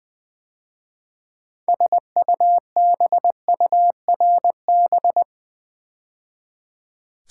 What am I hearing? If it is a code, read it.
Morse code, heard as a keyed tone, about 20 wpm, 700 Hz. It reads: SUBURB
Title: 6 Letter Words - First 3 Letters Given - 20wpm